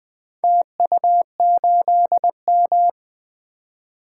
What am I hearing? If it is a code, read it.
Morse code, heard as a keyed tone, about 20 wpm, 700 Hz. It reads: TU8M
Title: Four Letter-Number Combinations - 20wpm